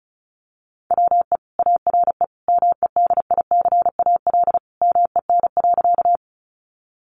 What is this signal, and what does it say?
Morse code, heard as a keyed tone, about 34 wpm, 700 Hz. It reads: WE ARE MEDICAL MEN.
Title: Sentences from Top 800 Words 35wpm